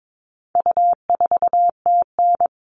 Morse code, heard as a keyed tone, about 22 wpm, 700 Hz. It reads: U4TN
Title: All Cut Numbers - 4-Digit 22wpm